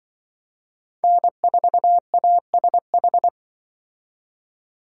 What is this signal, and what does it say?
Morse code, heard as a keyed tone, about 24 wpm, 700 Hz. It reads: N4ASH
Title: Speed Racing - USA 1x3 Call Signs 20wpm